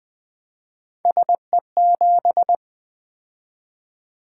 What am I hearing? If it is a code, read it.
Morse code, heard as a keyed tone, about 20 wpm, 700 Hz. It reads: SE7
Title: Three Letter-Number Combinations - 20wpm